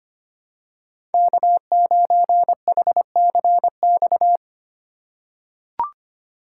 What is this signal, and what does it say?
Morse code, heard as a keyed tone, about 25 wpm, 700 Hz. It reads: K9HCX
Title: USA Call Signs - 5 Characters - 4x Character Spacing - 25wpm